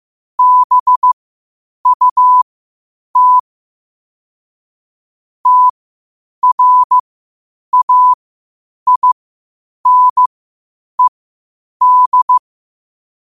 Morse code, heard as a keyed tone, about 15 wpm, spaced as for 8 wpm, 1000 Hz. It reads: BUT TRAINED
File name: qotdfiles / 8wpm